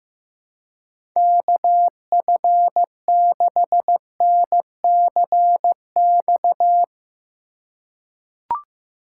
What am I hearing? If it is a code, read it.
Morse code, heard as a keyed tone, about 16 wpm, 700 Hz. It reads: KF6NCX
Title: K1USN SST QSO Practice - 15wpm